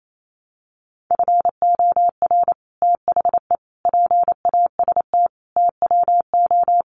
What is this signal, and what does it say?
Morse code, heard as a keyed tone, about 28 wpm, 700 Hz. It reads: FOR THE PAST TWO
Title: Sets of 5 Words Encore2 28wpm